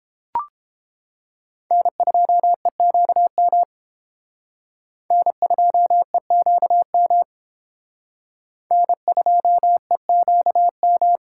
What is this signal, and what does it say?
Morse code, heard as a keyed tone, about 28 wpm, 700 Hz. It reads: N2EQM N2EQM N2EQM